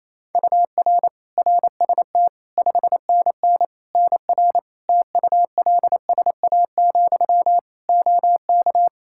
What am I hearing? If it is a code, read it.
Morse code, heard as a keyed tone, about 28 wpm, 700 Hz. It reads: UR RST 5NN NR TULSA, OK
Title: Practice QSO - Exchange 1 - Beginning - 28wpm